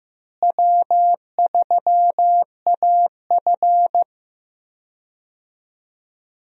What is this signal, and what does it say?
Morse code, heard as a keyed tone, about 15 wpm, 700 Hz. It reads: W3AF
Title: USA Call Signs - 4 Characters - Repeated 2x - 15wpm